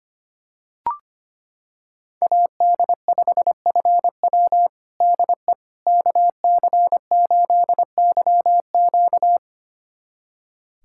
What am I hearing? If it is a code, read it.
Morse code, heard as a keyed tone, about 25 wpm, 700 Hz. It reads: AD5FW DE KC8YQ
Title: USA Callsign DE Callsign - 5 Characters - 25wpm